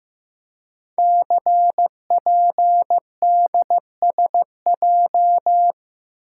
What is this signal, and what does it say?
Morse code, heard as a keyed tone, about 15 wpm, 700 Hz. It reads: CPDSJ
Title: Code Groups - 5 Letter:Numbers - 15wpm